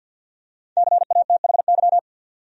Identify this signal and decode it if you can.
Morse code, heard as a keyed tone, about 52 wpm, 700 Hz. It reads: CATHY
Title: USA Common Names - Max 5 Letters - 50wpm